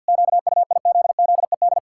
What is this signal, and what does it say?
Morse code, heard as a keyed tone, about 47 wpm, 700 Hz. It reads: QUIZZED
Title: Pangram Challenge v2 - 50wpm